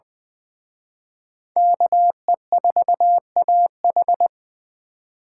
Morse code, heard as a keyed tone, about 24 wpm, 700 Hz. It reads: KE4AH
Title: Speed Racing - USA 2x2 Call Signs 20wpm